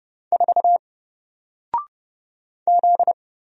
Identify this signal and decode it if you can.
Morse code, heard as a keyed tone, about 30 wpm, 700 Hz. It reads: 4 Z
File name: Single-Letter-Number-30wpm